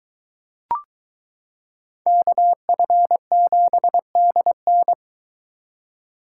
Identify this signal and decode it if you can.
Morse code, heard as a keyed tone, about 22 wpm, 700 Hz. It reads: KF7DN